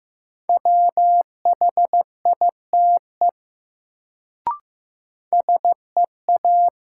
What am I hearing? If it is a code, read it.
Morse code, heard as a keyed tone, about 15 wpm, 700 Hz. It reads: WHITE SEA
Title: Sentences from Top 300 Words - New Word Drill - 15wpm